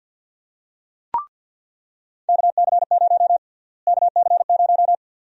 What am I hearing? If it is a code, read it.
Morse code, heard as a keyed tone, about 48 wpm, 700 Hz. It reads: KC0 KC0
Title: USA Call Sign Prefixes - Repeated 2x - 50wpm